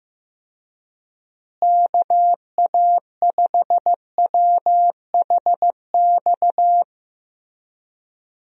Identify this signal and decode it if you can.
Morse code, heard as a keyed tone, about 15 wpm, 700 Hz. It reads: KA5WHX
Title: USA Call Signs - 6 Characters - Repeated 3x - 15wpm